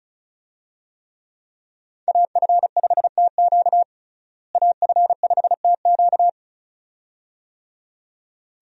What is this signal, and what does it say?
Morse code, heard as a keyed tone, about 36 wpm, 700 Hz. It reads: AF5TQ AF5TQ
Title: USA Call Signs - 5 Characters - Repeated 2x - 35wpm